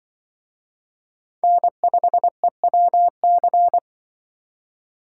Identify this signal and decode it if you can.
Morse code, heard as a keyed tone, about 23 wpm, 700 Hz. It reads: N5EWC
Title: Speed Racing - USA 1x3 Call Signs 20wpm